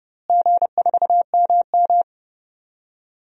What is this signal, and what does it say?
Morse code, heard as a keyed tone, about 30 wpm, 700 Hz. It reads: G4MM
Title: DXCC-100 Call Signs - 4 Characters - Repeated 2x - 30wpm